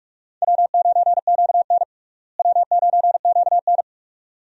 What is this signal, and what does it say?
Morse code, heard as a keyed tone, about 42 wpm, 700 Hz. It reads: W9QN W9QN
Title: USA Call Signs - 4 Characters - Repeated 2x - 45wpm